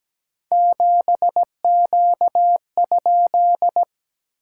Morse code, heard as a keyed tone, about 17 wpm, 700 Hz. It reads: 7Q?
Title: SOTA QSO Practice 17wpm